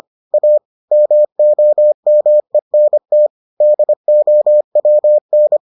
Morse code, heard as a keyed tone, about 25 wpm, 600 Hz. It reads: A MOMENT DOWN